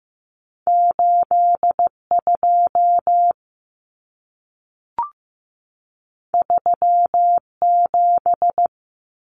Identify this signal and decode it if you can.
Morse code, heard as a keyed tone, about 15 wpm, 700 Hz. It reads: 82 37
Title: Numbers - 2-Digit 15wpm